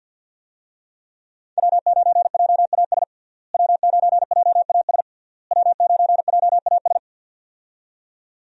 Morse code, heard as a keyed tone, about 48 wpm, 700 Hz. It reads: W9JAS W9JAS W9JAS
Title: USA Call Signs - 5 Characters - Repeated 3x - 50wpm